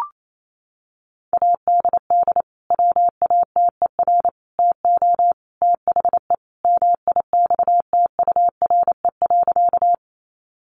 Morse code, heard as a keyed tone, about 28 wpm, 700 Hz. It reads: ADD WATER TO THE MIXTURE.
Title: Sentences from Top 1500 Words 28wpm